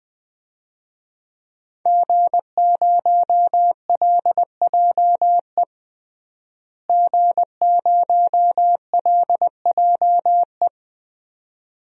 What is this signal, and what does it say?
Morse code, heard as a keyed tone, about 21 wpm, 700 Hz. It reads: G0LJE G0LJE
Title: DXCC-100 Call Signs - 5 Characters - Repeated 2x - 20wpm